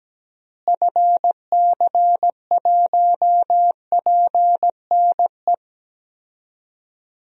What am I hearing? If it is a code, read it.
Morse code, heard as a keyed tone, about 17 wpm, 700 Hz. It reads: FC1PNE
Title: DXCC-100 Call Signs - 6 Characters - Repeated 2x - 17wpm